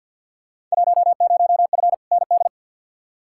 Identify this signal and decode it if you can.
Morse code, heard as a keyed tone, about 49 wpm, 700 Hz. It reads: 10F ND
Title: Field Day Exchanges Encore - 50wpm